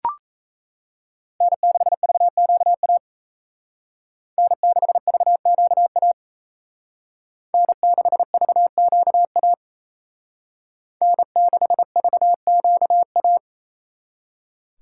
Morse code, heard as a keyed tone, about 32 wpm, 700 Hz. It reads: N6VQA N6VQA N6VQA N6VQA
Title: Speed Racing - USA 1x3 Call Signs 28wpm